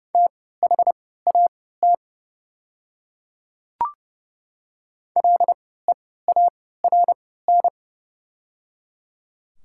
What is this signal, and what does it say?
Morse code, heard as a keyed tone, about 30 wpm, spaced as for 17 wpm, 700 Hz. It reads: THAT LEARN
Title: 3x Character Spacing - Top 300 Words - 30wpm